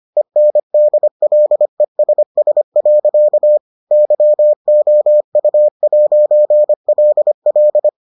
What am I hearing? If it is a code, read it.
Morse code, heard as a keyed tone, about 25 wpm, 600 Hz. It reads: ENDLESS. YOU'LL